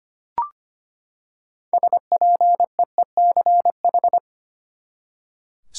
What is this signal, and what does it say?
Morse code, heard as a keyed tone, about 25 wpm, 700 Hz. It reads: SPEECH
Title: Six Letter Words - 25wpm